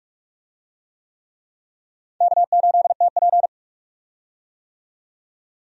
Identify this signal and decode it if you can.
Morse code, heard as a keyed tone, about 46 wpm, 700 Hz. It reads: K8TP
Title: USA Call Signs - 4 Characters - 45wpm